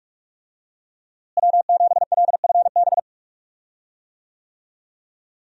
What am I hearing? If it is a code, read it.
Morse code, heard as a keyed tone, about 46 wpm, 700 Hz. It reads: W7LFB
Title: USA Call Signs - 5 Characters - Repeated 3x - 45wpm